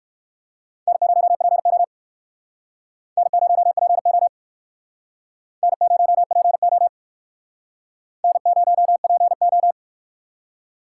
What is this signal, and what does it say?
Morse code, heard as a keyed tone, about 59 wpm, 700 Hz. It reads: N0PO N0PO N0PO N0PO